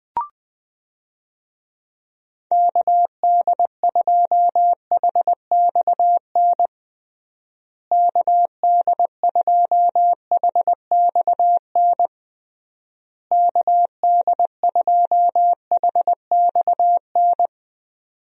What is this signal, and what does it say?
Morse code, heard as a keyed tone, about 20 wpm, 700 Hz. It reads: KD2HXN KD2HXN KD2HXN
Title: USA Call Signs - 6 Characters - Repeated 3x - 20wpm